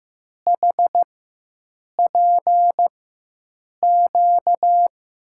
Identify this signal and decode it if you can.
Morse code, heard as a keyed tone, about 15 wpm, spaced as for 7 wpm, 700 Hz. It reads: HPQ
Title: USA Call Signs - 4x Character Spacing - 15wpm